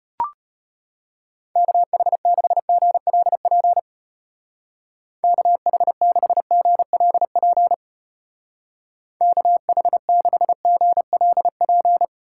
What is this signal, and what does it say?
Morse code, heard as a keyed tone, about 31 wpm, 700 Hz. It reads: KH6GLP KH6GLP KH6GLP
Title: Speed Racing - USA 2x3 Call Signs 25wpm